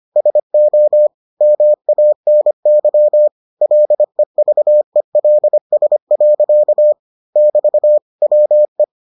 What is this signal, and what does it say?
Morse code, heard as a keyed tone, about 25 wpm, 600 Hz. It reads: SO MANY LEVELS. <BT> WE